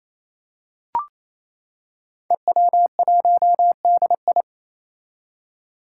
Morse code, heard as a keyed tone, about 28 wpm, 700 Hz. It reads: EW1DI
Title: Five Challenging Letter-Number Combinations 28wpm